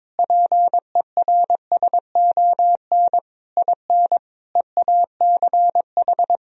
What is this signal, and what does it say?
Morse code, heard as a keyed tone, about 21 wpm, 700 Hz. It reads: PERSON IN EACH
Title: Sentences from Top 1000 Words - Review Part 1 - 22wpm